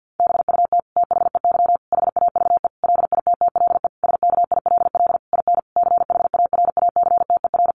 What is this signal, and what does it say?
Morse code, heard as a keyed tone, about 50 wpm, 700 Hz. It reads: BUT THEY HAVE LITTLE SKILL IN CHARACTER
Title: Sentences from Top 1200 Words 50wpm